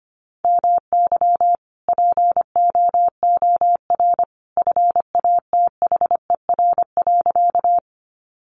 Morse code, heard as a keyed tone, about 25 wpm, 700 Hz. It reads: MY POOR FATHER.
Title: Sentences from Top 1000 Words 25wpm